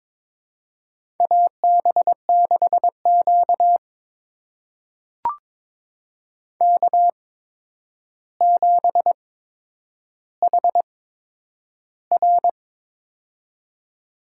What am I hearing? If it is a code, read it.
Morse code, heard as a keyed tone, about 22 wpm, spaced as for 9 wpm, 700 Hz. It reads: AB6Q K7HR
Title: USA Call Signs - 8x Character Spacing - 22wpm